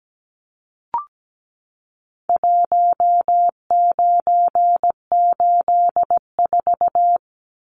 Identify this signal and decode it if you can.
Morse code, heard as a keyed tone, about 17 wpm, 700 Hz. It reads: 1984
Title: Numbers - Contest Style - 4-Digit 17wpm